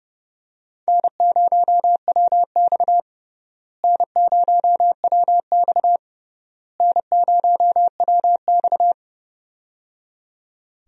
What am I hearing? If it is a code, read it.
Morse code, heard as a keyed tone, about 30 wpm, 700 Hz. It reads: N0WX N0WX N0WX
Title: USA Call Signs - 4 Characters - Repeated 3x - 30wpm